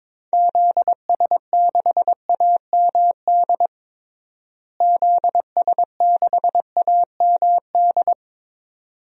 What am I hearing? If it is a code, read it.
Morse code, heard as a keyed tone, about 22 wpm, 700 Hz. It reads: ZS6AMD ZS6AMD
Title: DXCC-100 Call Signs - 6 Characters - Repeated 2x - 22wpm